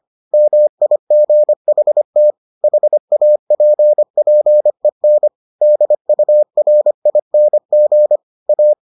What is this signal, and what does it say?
Morse code, heard as a keyed tone, about 25 wpm, 600 Hz. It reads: MIGHT HAPPEN DURING A